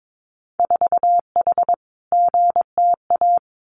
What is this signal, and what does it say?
Morse code, heard as a keyed tone, about 22 wpm, 700 Hz. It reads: 4H GTA
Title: Winter Field Day Exchanges Encore 22wpm